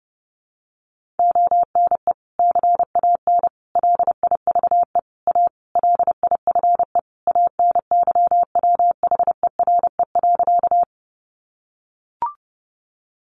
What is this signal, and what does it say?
Morse code, heard as a keyed tone, about 30 wpm, 700 Hz. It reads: ONE CAN LIVE A LIFE ANYWHERE.
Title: Sentences from Top 1400 Words 30wpm